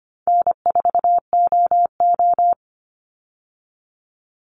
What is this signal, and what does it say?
Morse code, heard as a keyed tone, about 25 wpm, 700 Hz. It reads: N4OO
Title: Easy Call Signs Repeated 3x - 25wpm